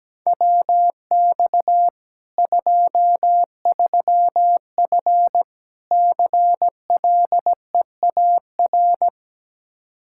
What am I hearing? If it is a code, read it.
Morse code, heard as a keyed tone, about 17 wpm, 700 Hz. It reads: WX 23F CLEAR
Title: K1USN SST QSO Practice 17wpm